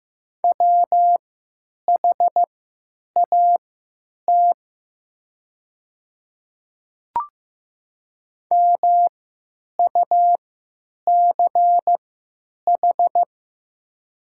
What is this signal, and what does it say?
Morse code, heard as a keyed tone, about 15 wpm, spaced as for 9 wpm, 700 Hz. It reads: WHAT MUCH
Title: 3x Character Spacing - Top 200 Words - 15wpm